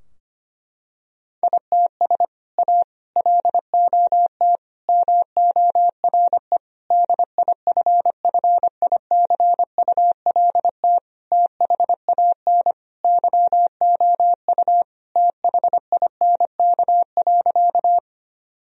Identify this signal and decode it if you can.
Morse code, heard as a keyed tone, about 25 wpm, 700 Hz. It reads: ITS A LOT MORE DIFFICULT THAN YOU THINK.
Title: Sentences from Top 1000 Words - Review Part 3 - 25wpm